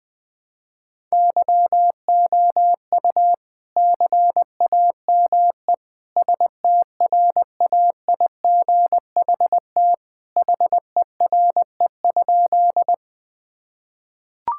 Morse code, heard as a keyed tone, about 20 wpm, 700 Hz. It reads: YOU CAME STRAIGHT HERE?
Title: Sentences from Top 1000 Words - Review Part 6 - 20wpm